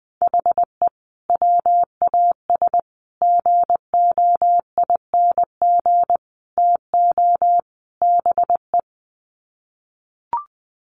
Morse code, heard as a keyed tone, about 20 wpm, 700 Hz. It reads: HE WAS GOING TO BE
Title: Sets of 5 Words 20wpm